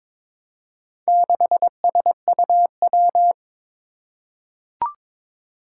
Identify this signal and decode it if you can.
Morse code, heard as a keyed tone, about 22 wpm, 700 Hz. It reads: 6SUW
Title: Four Challenging Letter-Number Combinations 22wpm